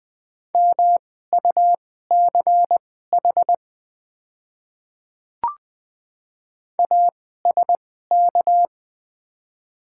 Morse code, heard as a keyed tone, about 20 wpm, spaced as for 14 wpm, 700 Hz. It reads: MUCH ASK
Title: 2x Character Spacing - Top 200 Words - 20wpm